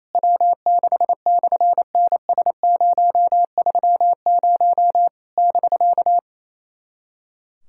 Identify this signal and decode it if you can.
Morse code, heard as a keyed tone, about 28 wpm, 700 Hz. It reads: W6/NS030 <BK>
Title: SOTA QSO Practice 28wpm